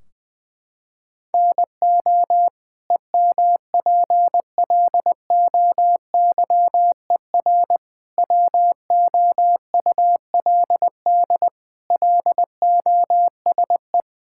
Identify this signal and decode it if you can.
Morse code, heard as a keyed tone, about 21 wpm, 700 Hz. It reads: NO EMPLOYER WOULD LOSE
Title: Sentences from Top 1600 Words 20wpm